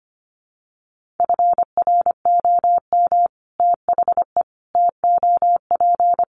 Morse code, heard as a keyed tone, about 25 wpm, 700 Hz. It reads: FROM THE TOP
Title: Sets of 5 Words Encore 25wpm